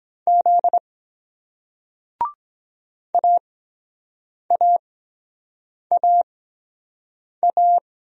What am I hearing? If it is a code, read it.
Morse code, heard as a keyed tone, about 23 wpm, spaced as for 7 wpm, 700 Hz. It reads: Z AAAA